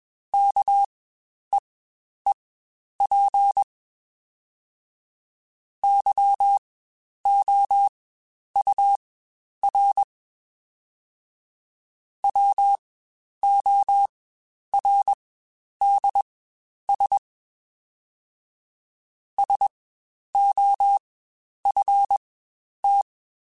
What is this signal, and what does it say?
Morse code, heard as a keyed tone, about 21 wpm, spaced as for 10 wpm, 800 Hz. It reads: KEEP YOUR WORDS SOFT
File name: Spaced / Chuckles Spaced-mp3s / SoftWords sp @20